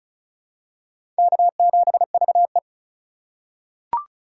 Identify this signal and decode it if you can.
Morse code, heard as a keyed tone, about 36 wpm, 700 Hz. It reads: K7VE
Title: USA Call Signs - 3x Character Spacing - 35wpm